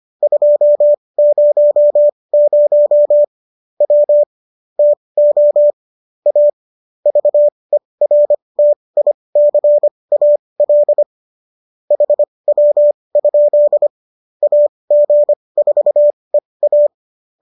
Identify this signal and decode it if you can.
Morse code, heard as a keyed tone, about 25 wpm, spaced as for 20 wpm, 590 Hz. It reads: 200 W TO A VERTICAL HW? AG4EA